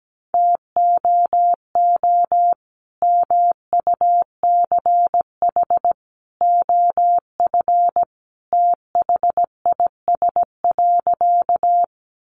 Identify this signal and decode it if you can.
Morse code, heard as a keyed tone, about 17 wpm, 700 Hz. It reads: TOO MUCH OF THIS.
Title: Sentences from Top 200 Words 17wpm